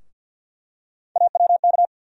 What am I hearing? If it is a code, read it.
Morse code, heard as a keyed tone, about 50 wpm, 700 Hz. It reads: AWK
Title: USA Call Sign Postfixes - 3 Letters 50wpm